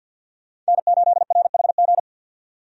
Morse code, heard as a keyed tone, about 50 wpm, 700 Hz. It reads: N8RHG
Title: USA Call Signs - 5 Characters - 50wpm